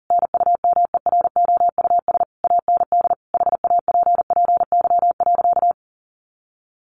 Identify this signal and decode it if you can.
Morse code, heard as a keyed tone, about 40 wpm, 700 Hz. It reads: NUMEROUS AND HAPPY.
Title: Sentences from Top 1400 Words 40wpm